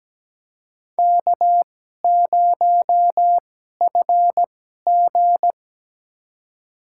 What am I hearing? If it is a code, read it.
Morse code, heard as a keyed tone, about 17 wpm, spaced as for 12 wpm, 700 Hz. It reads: K0FG
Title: USA Call Signs - 4 Characters - 2x Character Spacing - 17wpm